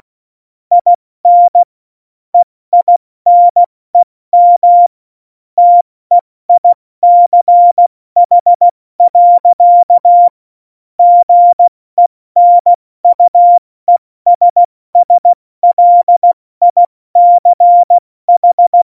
Morse code, heard as a keyed tone, about 16 wpm, 700 Hz. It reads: IN EINEM TEICH. GENUESSLICH
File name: Morsekurs / Witze / Witz 04 August 2020, 70 F80